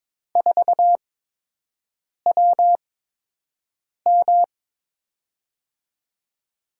Morse code, heard as a keyed tone, about 21 wpm, spaced as for 6 wpm, 700 Hz. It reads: 4WM
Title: USA Call Signs - 5 Characters - 8x Character Spacing - 22wpm